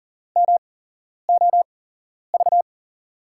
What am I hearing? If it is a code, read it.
Morse code, heard as a keyed tone, about 40 wpm, spaced as for 11 wpm, 700 Hz. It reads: MOV
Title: USA Call Signs - 6 Characters - 8x Character Spacing - 40wpm